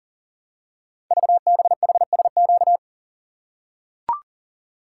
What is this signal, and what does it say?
Morse code, heard as a keyed tone, about 40 wpm, 700 Hz. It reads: VBHSQ
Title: Code Groups - 5 Letter:Numbers - 40wpm